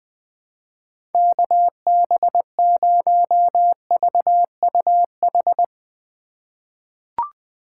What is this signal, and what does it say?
Morse code, heard as a keyed tone, about 20 wpm, 700 Hz. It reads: KB0VUH
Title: USA Call Signs - 8x Character Spacing - 20wpm